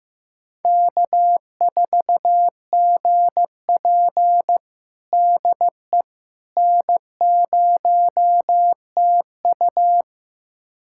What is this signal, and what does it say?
Morse code, heard as a keyed tone, about 16 wpm, 700 Hz. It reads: K4GP DE N0TU
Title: USA Callsign DE Callsign - 4 Characters - 15wpm